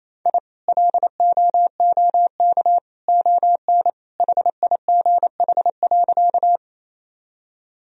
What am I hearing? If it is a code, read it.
Morse code, heard as a keyed tone, about 28 wpm, 700 Hz. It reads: I LOOK ON HIGH.
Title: Sentences from Top 1000 Words - Review Part 1 - 28wpm